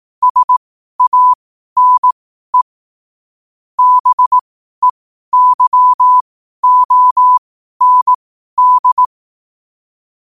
Morse code, heard as a keyed tone, about 18 wpm, spaced as for 12 wpm, 1000 Hz. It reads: SANE BEYOND
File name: qotdfiles / 12wpm